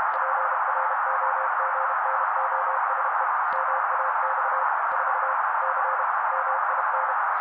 Morse code, heard as a keyed tone, about 32 wpm, 530 Hz. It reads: PROGNOSE MAX 4 C MIN
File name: daily / 31wpm noise